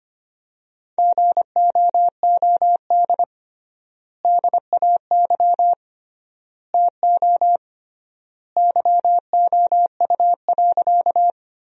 Morse code, heard as a keyed tone, about 25 wpm, 700 Hz. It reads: GOOD DAY TO YOU.
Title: 3x Word-Spacing - Sentences from Top 200 Words - 25wpm